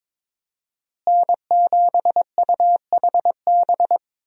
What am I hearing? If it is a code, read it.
Morse code, heard as a keyed tone, about 22 wpm, 700 Hz. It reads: N7UHB
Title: USA Call Signs - 5 Characters - Repeated 3x - 22wpm